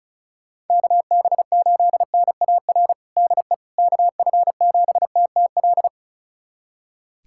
Morse code, heard as a keyed tone, about 36 wpm, 700 Hz. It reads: KB8NAR DE KF7TTL
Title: USA Callsign DE Callsign - 6 Characters - 35wpm